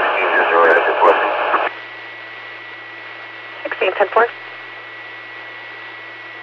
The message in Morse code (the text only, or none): none